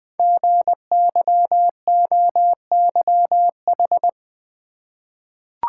Morse code, heard as a keyed tone, about 20 wpm, 700 Hz. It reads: GYOYH
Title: Code Groups - 5 Letter:Numbers - 20wpm